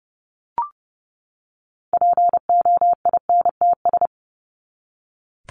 Morse code, heard as a keyed tone, about 30 wpm, 700 Hz. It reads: POINTS
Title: ES:S Suffix - Top 100 Words with Base Words 30wpm